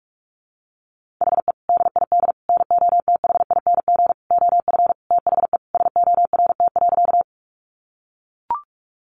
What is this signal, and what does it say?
Morse code, heard as a keyed tone, about 44 wpm, 700 Hz. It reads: HE DID NOTHING OF THE SORT.
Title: Sentences from Top 1300 Words 45wpm